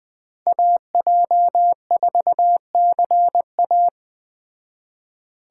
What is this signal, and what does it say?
Morse code, heard as a keyed tone, about 20 wpm, 700 Hz. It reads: AJ4CA
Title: Speed Racing - USA 2x2 Call Signs 17wpm